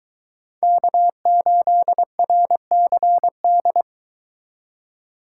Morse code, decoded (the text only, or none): K8RCD